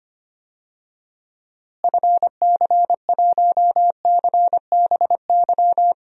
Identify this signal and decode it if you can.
Morse code, heard as a keyed tone, about 25 wpm, 700 Hz. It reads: FC1CBY
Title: DXCC-100 Call Signs - 6 Characters - Repeated 2x - 25wpm